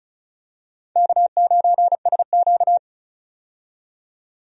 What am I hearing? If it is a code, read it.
Morse code, heard as a keyed tone, about 36 wpm, 700 Hz. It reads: K9SQ
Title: USA Call Signs - 4 Characters - 35wpm